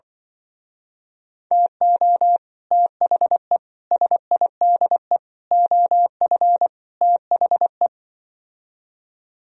Morse code, heard as a keyed tone, about 25 wpm, 700 Hz. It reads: TO THE SIDE OF THE